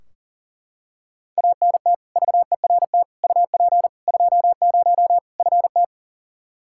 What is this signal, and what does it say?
Morse code, heard as a keyed tone, about 40 wpm, 700 Hz. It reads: ANT VERT UP 20 FT
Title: Practice QSO - Exchange 2 - Info - 40wpm